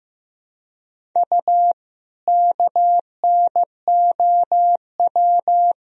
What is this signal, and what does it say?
Morse code, heard as a keyed tone, about 15 wpm, 700 Hz. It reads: U KNOW
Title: Sentences from Top 500 Words and Abbreviations 15wpm